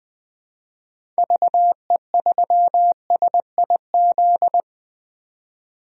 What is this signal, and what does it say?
Morse code, heard as a keyed tone, about 20 wpm, 700 Hz. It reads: VE3SIZ
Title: DXCC-100 Call Signs - 6 Characters - 20wpm